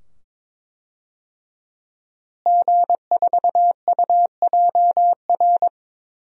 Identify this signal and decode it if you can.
Morse code, heard as a keyed tone, about 22 wpm, 700 Hz. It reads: G4UJR
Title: DXCC-100 Call Signs - 5 Characters - Repeated 2x - 22wpm